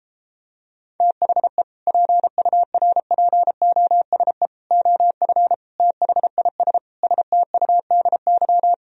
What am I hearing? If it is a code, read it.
Morse code, heard as a keyed tone, about 33 wpm, 700 Hz. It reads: THE PURPOSE OF THIS STUDY